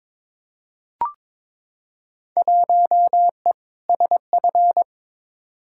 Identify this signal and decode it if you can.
Morse code, heard as a keyed tone, about 22 wpm, 700 Hz. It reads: 1E SF
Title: Field Day Exchanges - 22wpm